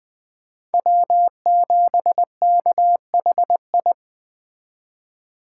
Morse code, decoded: W7KHI

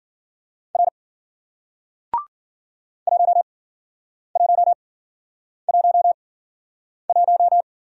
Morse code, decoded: R 1111

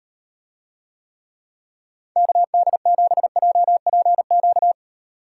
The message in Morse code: KD7JPQ

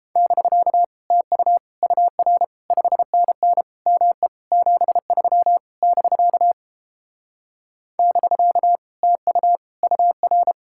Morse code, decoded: <BK> TU UR 5NN ME 73 <BK> <BK> TU UR

